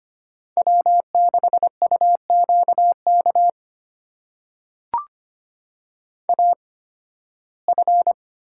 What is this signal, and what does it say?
Morse code, heard as a keyed tone, about 25 wpm, spaced as for 16 wpm, 700 Hz. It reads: W6UQK AF
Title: USA Call Signs - 5 Characters - 8x Character Spacing - 25wpm